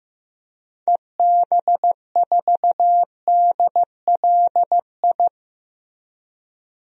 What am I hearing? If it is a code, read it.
Morse code, heard as a keyed tone, about 15 wpm, 700 Hz. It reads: EB4DLI
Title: DXCC-100 Call Signs - 6 Characters - Repeated 2x - 15wpm